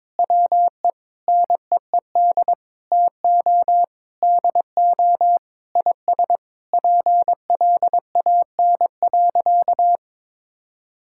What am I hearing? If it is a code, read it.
Morse code, heard as a keyed tone, about 22 wpm, 700 Hz. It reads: WE NEED TO DO IS PLAN.
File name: practice / 22wpm / Sentences with 8-Word Phrases - 22wpm